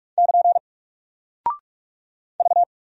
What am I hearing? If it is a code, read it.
Morse code, heard as a keyed tone, about 44 wpm, 700 Hz. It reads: <KN> V
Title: Single Character (Letter:Number:Punctuation:Prosign) 45wpm